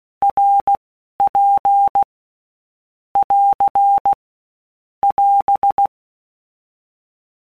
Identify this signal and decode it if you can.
Morse code, heard as a keyed tone, about 16 wpm, 800 Hz. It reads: R P <AR> <AS>
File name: Basic Letter Groups By Blocks / Groups@15 / RKPX@15